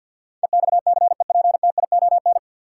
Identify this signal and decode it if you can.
Morse code, heard as a keyed tone, about 52 wpm, 700 Hz. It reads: EXCEPTION
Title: Sets of 4 Words - New Word Drill - 50wpm